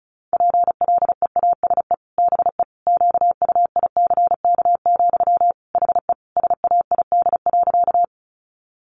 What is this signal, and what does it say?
Morse code, heard as a keyed tone, about 34 wpm, 700 Hz. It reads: PLEASE BE QUICK, HE SAID.
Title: Sentences from Top 1500 Words 35wpm